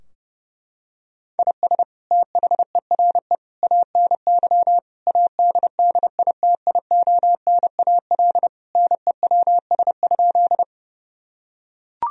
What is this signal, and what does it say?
Morse code, heard as a keyed tone, about 32 wpm, 700 Hz. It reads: IS THERE ANY ADDITIONAL NEWS?
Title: Sentences from Top 1000 Words - Review Part 4 - 30wpm